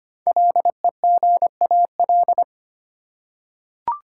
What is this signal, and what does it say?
Morse code, heard as a keyed tone, about 25 wpm, 700 Hz. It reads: LEGAL
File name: 5 Letter Words - First 3 Letters Given - 25wpm